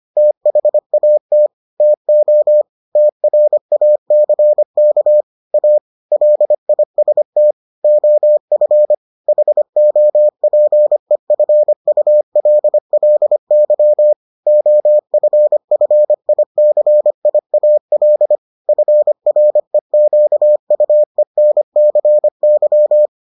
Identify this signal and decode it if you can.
Morse code, heard as a keyed tone, about 25 wpm, 600 Hz. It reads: THAT TO TRACK A LIST OF HOPEFULLY OFFICIAL FREQUENCY